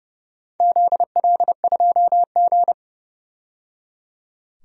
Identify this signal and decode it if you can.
Morse code, heard as a keyed tone, about 30 wpm, 700 Hz. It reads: ZL2G
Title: DXCC-100 Call Signs - 4 Characters - 30wpm